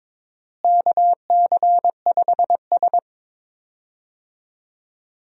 Morse code, heard as a keyed tone, about 22 wpm, 700 Hz. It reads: KC5S
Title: USA Call Signs - 4 Characters - Repeated 2x - 22wpm